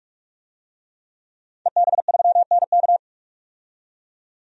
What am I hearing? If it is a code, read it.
Morse code, heard as a keyed tone, about 44 wpm, 700 Hz. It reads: EB3NK
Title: DXCC-100 Call Signs - 5 Characters - 45wpm